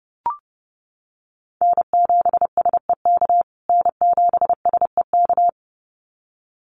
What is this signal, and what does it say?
Morse code, heard as a keyed tone, about 30 wpm, 700 Hz. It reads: N7SEK N7SEK
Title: USA Call Signs Repeated 2x 30wpm